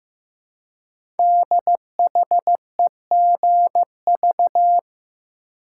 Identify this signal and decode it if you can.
Morse code, heard as a keyed tone, about 16 wpm, 700 Hz. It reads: DHEGV
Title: Code Groups - 5 Letters - 15wpm